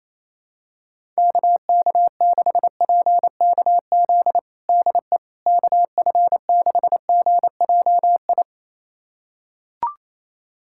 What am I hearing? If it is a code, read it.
Morse code, heard as a keyed tone, about 28 wpm, 700 Hz. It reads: KK6PKZ DE KF6GJI
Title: USA Callsign DE Callsign 28wpm